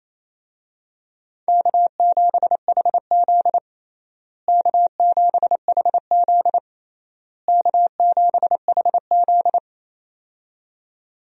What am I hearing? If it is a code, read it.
Morse code, heard as a keyed tone, about 28 wpm, 700 Hz. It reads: K7HZ K7HZ K7HZ
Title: USA Call Signs - 4 Characters - Repeated 3x - 28wpm